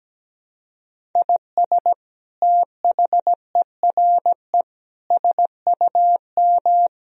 Morse code, heard as a keyed tone, about 17 wpm, 700 Hz. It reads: IS THERE SUM
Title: Sentences from Top 200 Words and Abbreviations 17wpm